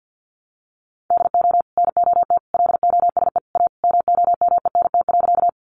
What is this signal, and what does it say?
Morse code, heard as a keyed tone, about 50 wpm, 700 Hz. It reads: DO NOT LOSE A MOMENT.
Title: Sentences from Top 1100 Words 50wpm